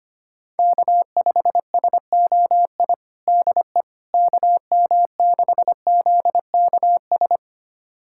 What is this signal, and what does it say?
Morse code, heard as a keyed tone, about 25 wpm, 700 Hz. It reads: K5SOI DE KM6ZKS
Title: USA Callsign DE Callsign 25wpm